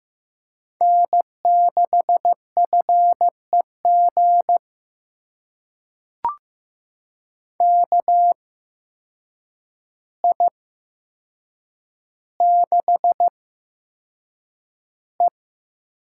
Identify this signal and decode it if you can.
Morse code, heard as a keyed tone, about 15 wpm, spaced as for 7 wpm, 700 Hz. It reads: N6FEG KI6E
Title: USA Call Signs - 5 Characters - 8x Character Spacing - 15wpm